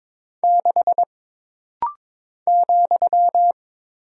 Morse code, heard as a keyed tone, about 22 wpm, 700 Hz. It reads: 6 ,
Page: This rapid-fire timing is geared to push your ICR (Instant Character Recognition) skills.